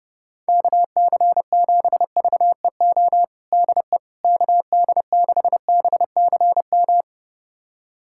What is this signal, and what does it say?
Morse code, heard as a keyed tone, about 30 wpm, 690 Hz. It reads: KC7VEO DE KD6BCM
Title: USA Callsign DE Callsign - 6 Characters - 30wpm